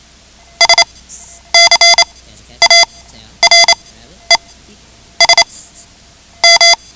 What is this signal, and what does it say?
Morse code, heard as a keyed tone, about 28 wpm, spaced as for 11 wpm, 2050 Hz. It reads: SCARES M
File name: ham radio / CodeQuick / Code Quick 10